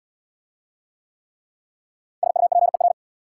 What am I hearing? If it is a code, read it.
Morse code, heard as a keyed tone, about 69 wpm, 690 Hz. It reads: SUPER